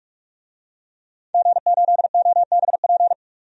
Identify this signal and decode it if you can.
Morse code, heard as a keyed tone, about 46 wpm, 690 Hz. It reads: G8OBP
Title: DXCC-100 Call Signs - 5 Characters - 45wpm